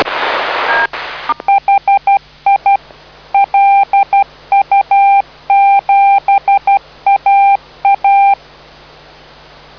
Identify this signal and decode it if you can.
Morse code, heard as a keyed tone, about 12 wpm, 790 Hz. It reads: HI LU7AA